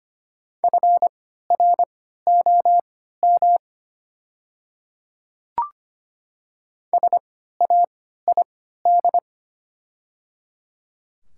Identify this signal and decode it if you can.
Morse code, heard as a keyed tone, about 25 wpm, spaced as for 14 wpm, 700 Hz. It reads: FROM SAID
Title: 3x Character Spacing - Top 300 Words - 25wpm